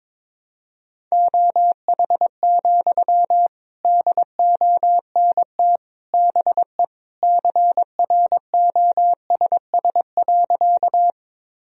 Morse code, heard as a keyed tone, about 22 wpm, 700 Hz. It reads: OH, DONT BE CROSS.
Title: Sentences from Top 1000 Words - Review Part 1 - 22wpm